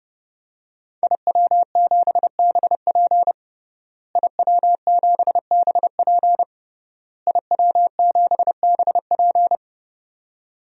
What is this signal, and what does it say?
Morse code, heard as a keyed tone, about 32 wpm, 700 Hz. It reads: IW7BP IW7BP IW7BP
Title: DXCC-100 Call Signs - 5 Characters - Repeated 3x - 30wpm